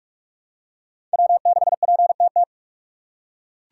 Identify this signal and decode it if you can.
Morse code, heard as a keyed tone, about 43 wpm, 700 Hz. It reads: W6PTT
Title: USA Call Signs - 5 Characters - 3x Character Spacing - 45wpm